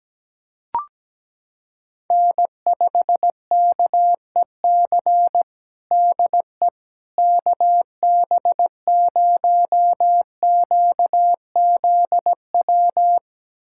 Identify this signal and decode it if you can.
Morse code, heard as a keyed tone, about 17 wpm, 700 Hz. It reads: N5KEC DE KB0QZW
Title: USA Callsign DE Callsign 17wpm